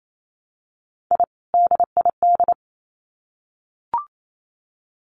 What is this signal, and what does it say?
Morse code, heard as a keyed tone, about 28 wpm, 700 Hz. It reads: I DID
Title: Sets of 2 Words 28wpm